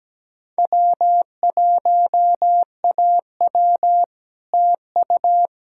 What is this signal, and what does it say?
Morse code, heard as a keyed tone, about 20 wpm, 700 Hz. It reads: W1AW TU
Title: Speed-Racing Common QSO Phrases Encore - 17wpm